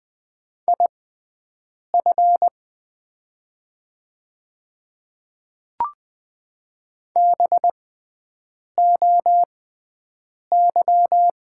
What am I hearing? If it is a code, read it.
Morse code, heard as a keyed tone, about 20 wpm, spaced as for 7 wpm, 700 Hz. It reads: IF BOY
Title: 6x Character Spacing - Top 500 Words - 20wpm